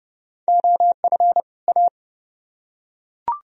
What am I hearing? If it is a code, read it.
Morse code, heard as a keyed tone, about 29 wpm, 700 Hz. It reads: OF A